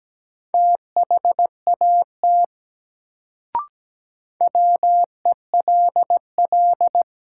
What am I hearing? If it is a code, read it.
Morse code, heard as a keyed tone, about 17 wpm, 700 Hz. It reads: THAT WELL
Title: Sets of 2 Words - New Word Drill - 17wpm